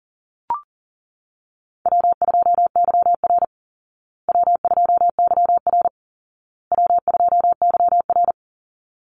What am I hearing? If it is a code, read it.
Morse code, heard as a keyed tone, about 40 wpm, 700 Hz. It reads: W2YR W2YR W2YR
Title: Easy Call Signs Repeated 3x - 40wpm